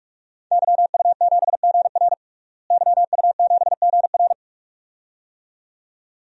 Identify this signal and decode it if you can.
Morse code, heard as a keyed tone, about 46 wpm, 700 Hz. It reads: YU7GR YU7GR
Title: DXCC-100 Call Signs - 5 Characters - Repeated 2x - 45wpm